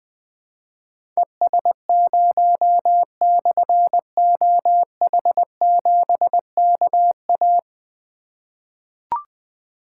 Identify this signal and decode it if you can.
Morse code, heard as a keyed tone, about 20 wpm, 700 Hz. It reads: ES0/OH7KA
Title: Difficult Call Signs 20wpm